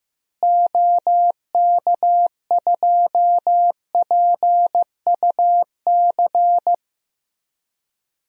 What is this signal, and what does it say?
Morse code, heard as a keyed tone, about 15 wpm, 700 Hz. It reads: OK2PUC
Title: DXCC-100 Call Signs - 6 Characters - Repeated 3x - 15wpm